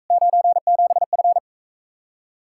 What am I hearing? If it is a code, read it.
Morse code, heard as a keyed tone, about 45 wpm, 700 Hz. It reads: 97F